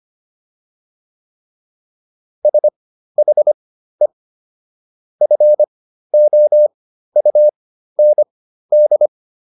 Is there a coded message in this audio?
Yes: Morse code from 2.5 to 9.1 s, heard as a keyed tone, about 25 wpm, spaced as for 13 wpm, 600 Hz. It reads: SHE FOUND